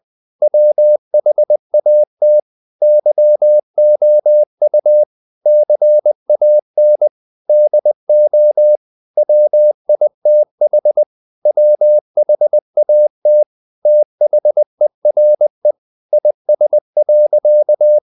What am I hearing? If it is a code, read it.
Morse code, heard as a keyed tone, about 20 wpm, 600 Hz. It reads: WHAT YOU CAN DO WITH WHAT THERE IS.
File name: the old man and the sea part 2 20wpm